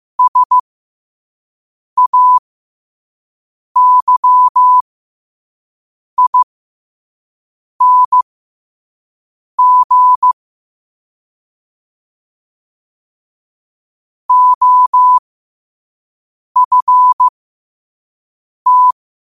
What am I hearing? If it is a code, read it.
Morse code, heard as a keyed tone, about 15 wpm, spaced as for 5 wpm, 1000 Hz. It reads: SAYING OFT
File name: qotdfiles / 5wpm